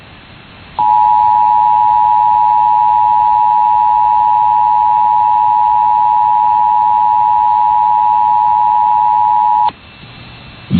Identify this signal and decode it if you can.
Emergency Alert System (SAME) tones (the 853 and 960 Hz attention tone). It is heard from 0.8 to 9.8 s.